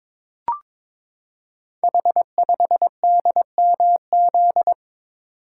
Morse code, heard as a keyed tone, about 22 wpm, 700 Hz. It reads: H5DMZ
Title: Code Groups - 5 Letter:Numbers - 22wpm